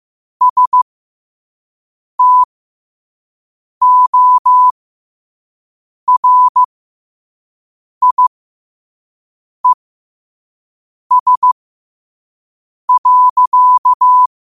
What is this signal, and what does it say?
Morse code, heard as a keyed tone, about 15 wpm, spaced as for 5 wpm, 1000 Hz. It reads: STORIES.